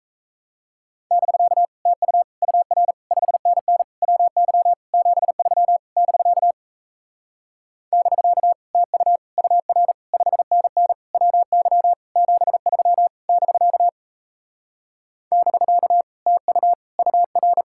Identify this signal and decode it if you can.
Morse code, heard as a keyed tone, about 37 wpm, 700 Hz. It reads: <BK> TU UR 5NN WY 73 <BK> <BK> TU UR 5NN WY 73 <BK> <BK> TU UR